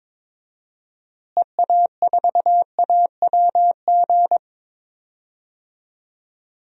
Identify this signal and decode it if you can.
Morse code, heard as a keyed tone, about 21 wpm, 700 Hz. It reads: EA4AWG
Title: DXCC-100 Call Signs - 6 Characters - Repeated 3x - 22wpm